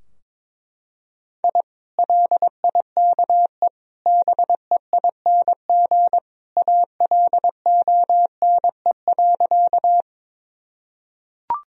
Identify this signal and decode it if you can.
Morse code, heard as a keyed tone, about 22 wpm, 700 Hz. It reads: I LIKE BEING ALONE.
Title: Sentences from Top 1000 Words - Review Part 3 - 22wpm